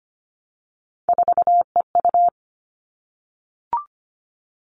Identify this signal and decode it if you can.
Morse code, heard as a keyed tone, about 25 wpm, 700 Hz. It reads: 4EU
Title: Three Challenging Letter-Number Combinations 25wpm